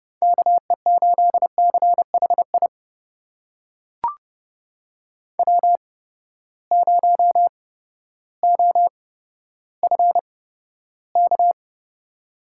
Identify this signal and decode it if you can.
Morse code, heard as a keyed tone, about 31 wpm, spaced as for 14 wpm, 700 Hz. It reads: KE8CHI W0OFK
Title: USA Call Signs - 8x Character Spacing - 30wpm